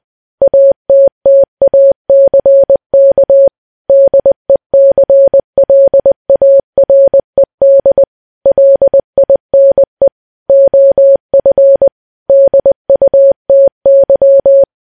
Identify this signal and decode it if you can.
Morse code, heard as a keyed tone, about 20 wpm, 550 Hz. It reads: ATTACK DECLARED LINE OF DUTY
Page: news-headlines-in-morse-code-at-20-wpm